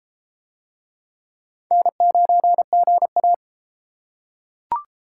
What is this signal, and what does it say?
Morse code, heard as a keyed tone, about 34 wpm, 700 Hz. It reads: N9GA